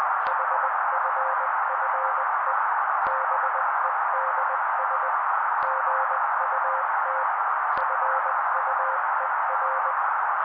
Morse code, heard as a keyed tone, about 20 wpm, 530 Hz, with noise. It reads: HFFE BEDS GUT FUER
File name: daily / 18wpm noise